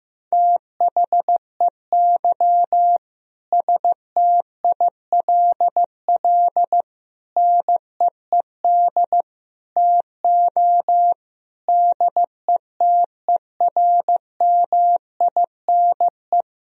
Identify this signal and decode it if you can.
Morse code, heard as a keyed tone, about 15 wpm, 700 Hz. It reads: THEY STILL NEED TO DETERMINE
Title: Sentences from Top 1000 Words - Review Part 4 - 15wpm